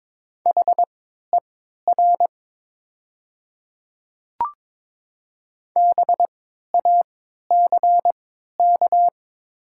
Morse code, heard as a keyed tone, about 22 wpm, spaced as for 13 wpm, 700 Hz. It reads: HER BACK
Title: 3x Character Spacing - Top 200 Words - 22wpm